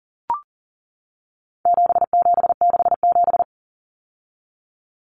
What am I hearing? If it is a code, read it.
Morse code, heard as a keyed tone, about 40 wpm, 700 Hz. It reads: 7767
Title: Numbers - 4-Digit 40wpm